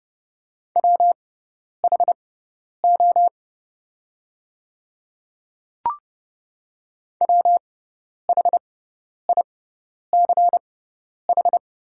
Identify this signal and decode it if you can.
Morse code, heard as a keyed tone, about 30 wpm, spaced as for 10 wpm, 700 Hz. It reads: WHO WHICH
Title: 6x Character Spacing - Top 100 Words - 30wpm